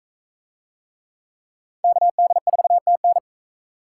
Morse code, heard as a keyed tone, about 40 wpm, 700 Hz. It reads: KD4TN